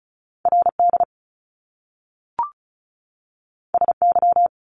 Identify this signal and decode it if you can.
Morse code, heard as a keyed tone, about 36 wpm, 700 Hz. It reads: RD SY
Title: Top Two Letter Combinations in Words 35wpm